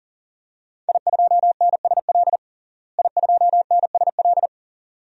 Morse code, heard as a keyed tone, about 40 wpm, 700 Hz. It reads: I2NSL I2NSL
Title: DXCC-100 Call Signs - 5 Characters - Repeated 2x - 40wpm